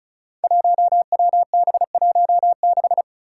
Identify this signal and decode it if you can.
Morse code, heard as a keyed tone, about 36 wpm, 700 Hz. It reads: 1WB16
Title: Five Challenging Letter-Number Combinations 35wpm